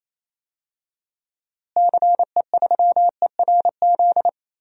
Morse code, heard as a keyed tone, about 28 wpm, 700 Hz. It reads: CE3ERZ
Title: DXCC-100 Call Signs - 6 Characters - Repeated 3x - 28wpm